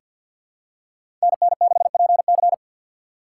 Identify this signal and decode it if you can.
Morse code, heard as a keyed tone, about 48 wpm, 700 Hz. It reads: NN6PC
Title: USA Call Signs - 5 Characters - 6x Character Spacing - 50wpm